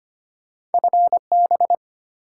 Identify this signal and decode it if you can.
Morse code, heard as a keyed tone, about 32 wpm, 700 Hz. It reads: FB